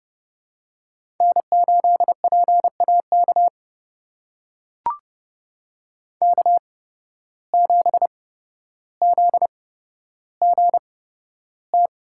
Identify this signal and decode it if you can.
Morse code, heard as a keyed tone, about 30 wpm, spaced as for 13 wpm, 700 Hz. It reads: N8PAK K7ZGT
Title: USA Call Signs - 5 Characters - 8x Character Spacing - 30wpm